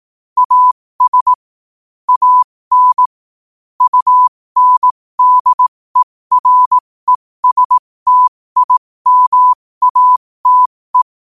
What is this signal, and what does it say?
Morse code, heard as a keyed tone, about 18 wpm, spaced as for 15 wpm, 1000 Hz. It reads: AS AN UNDERESTIMATE
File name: qotdfiles / 15wpm